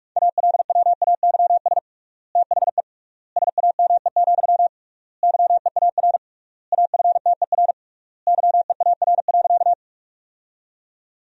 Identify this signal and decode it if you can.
Morse code, heard as a keyed tone, about 42 wpm, 700 Hz. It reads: ALWAYS THE SAME, YEAR AFTER YEAR.
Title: 3x Word-Spacing - Sentences from Top 300 Words - 45wpm